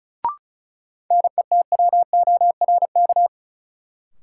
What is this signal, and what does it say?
Morse code, heard as a keyed tone, about 36 wpm, 700 Hz. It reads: NETWORK
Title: Sentences from Top 1000 Words - Review Part 7 - New Word Drill - 35wpm